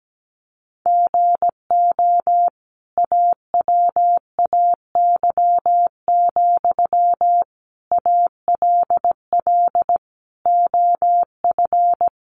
Morse code, heard as a keyed tone, about 17 wpm, 700 Hz. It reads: GO AWAY, ALL OF
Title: Sentences from Top 600 Words 17wpm